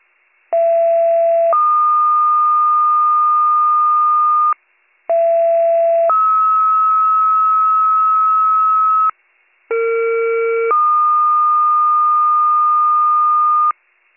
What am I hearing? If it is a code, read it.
Two-tone page, heard as a one-second tone, then a three-second tone, about 670 then 1220 Hz, about 670 then 1290 Hz, about 460 then 1220 Hz. It is heard from 0.5 to 13.7 s.